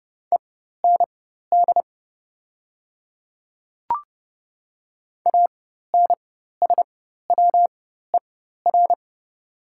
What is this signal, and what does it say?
Morse code, heard as a keyed tone, about 30 wpm, spaced as for 14 wpm, 700 Hz. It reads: END ANSWER
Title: 4x Character Spacing - Top 500 Words - 30wpm